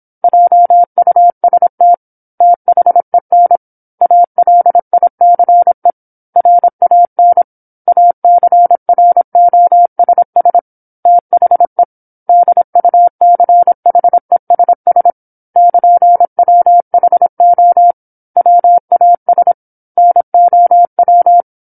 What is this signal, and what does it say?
Morse code, heard as a keyed tone, about 27 wpm, 700 Hz. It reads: JUST THEN ALICE RAN ACROSS THE DUCHESS <KN>WHO WAS NOW